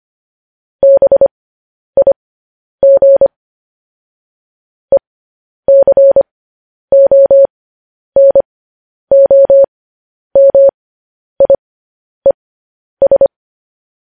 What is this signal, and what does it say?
Morse code, heard as a keyed tone, about 25 wpm, spaced as for 10 wpm, 550 Hz. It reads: BIG ECONOMIES
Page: News Headlines in Morse Code at 10 WPM